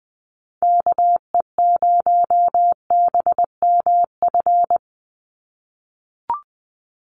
Morse code, heard as a keyed tone, about 20 wpm, 700 Hz. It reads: KE0BMF
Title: USA Call Signs 20wpm